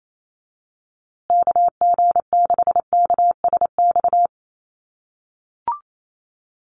Morse code, heard as a keyed tone, about 28 wpm, 700 Hz. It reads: KG6KSX
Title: USA Call Signs Repeated 3x 28wpm